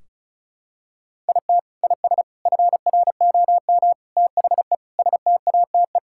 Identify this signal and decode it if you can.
Morse code, heard as a keyed tone, about 36 wpm, 700 Hz. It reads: IT IS FROM THE STATE
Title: Sentences from Top 1600 Words 35wpm